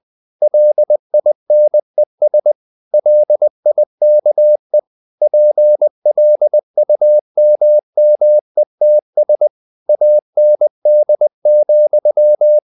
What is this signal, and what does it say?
Morse code, heard as a keyed tone, about 20 wpm, 600 Hz. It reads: LINES LIKE PLUMMETS AND,
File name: the old man and the sea / the old man and the sea part 1 20wpm